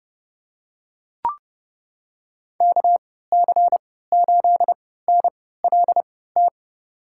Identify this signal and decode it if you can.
Morse code, heard as a keyed tone, about 31 wpm, spaced as for 17 wpm, 700 Hz. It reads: KC8NLT
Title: USA Call Signs - 6 Characters - 3x Character Spacing - 30wpm